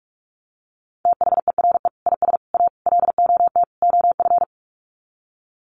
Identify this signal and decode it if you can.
Morse code, heard as a keyed tone, about 44 wpm, 700 Hz. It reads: THERE IS A LOT OF
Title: Sets of 5 Words 45wpm